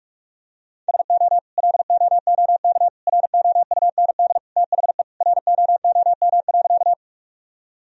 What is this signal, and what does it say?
Morse code, heard as a keyed tone, about 46 wpm, 700 Hz. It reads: SO LOOK ROUND THE ROOM.
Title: Sentences from Top 1000 Words - Review Part 1 - 45wpm